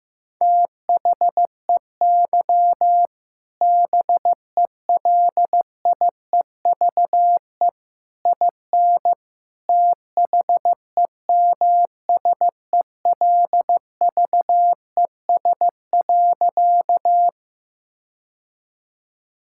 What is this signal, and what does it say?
Morse code, heard as a keyed tone, about 15 wpm, 700 Hz. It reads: THEY BELIEVE IN THEMSELVES.
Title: Sentences from Top 1000 Words - Review Part 5 - 15wpm